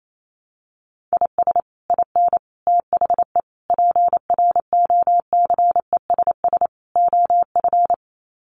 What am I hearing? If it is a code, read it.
Morse code, heard as a keyed tone, about 28 wpm, 700 Hz. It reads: IS IN THE PROCESS OF
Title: Sets of 5 Words Encore2 28wpm